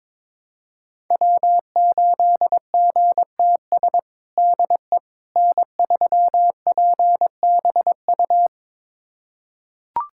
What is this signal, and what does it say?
Morse code, heard as a keyed tone, about 21 wpm, 700 Hz. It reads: W8GTS DE N3PBU
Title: USA Callsign DE Callsign - 5 Characters - 22wpm